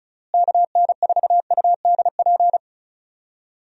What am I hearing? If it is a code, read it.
Morse code, heard as a keyed tone, about 36 wpm, 700 Hz. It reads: KN4UDP
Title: USA Call Signs - 6 Characters - Repeated 3x - 35wpm